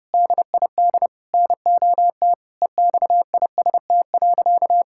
Morse code, heard as a keyed tone, about 32 wpm, 700 Hz. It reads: DID NOT EXIST.
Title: Sentences from Top 1000 Words - Review Part 6 - 30wpm